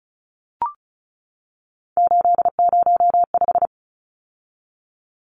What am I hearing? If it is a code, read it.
Morse code, heard as a keyed tone, about 34 wpm, 700 Hz. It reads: 805
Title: Numbers - 3-Digit 35wpm